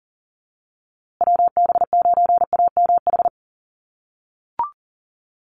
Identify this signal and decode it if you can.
Morse code, heard as a keyed tone, about 40 wpm, 700 Hz. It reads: WB9AMH
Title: USA Call Signs Repeated 3x 40wpm